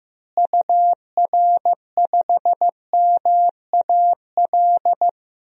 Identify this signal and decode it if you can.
Morse code, heard as a keyed tone, about 15 wpm, 700 Hz. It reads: UR5MAL
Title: DXCC-100 Call Signs - 6 Characters - Repeated 3x - 15wpm